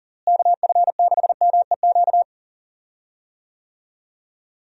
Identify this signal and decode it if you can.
Morse code, heard as a keyed tone, about 40 wpm, 700 Hz. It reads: KF6MEQ
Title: USA Call Signs - 6 Characters - Repeated 2x - 40wpm